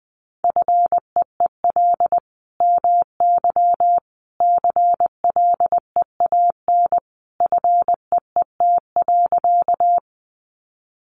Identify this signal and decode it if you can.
Morse code, heard as a keyed tone, about 20 wpm, 700 Hz. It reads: FEEL MY CLEAN FEET.
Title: Sentences from Top 1100 Words 20wpm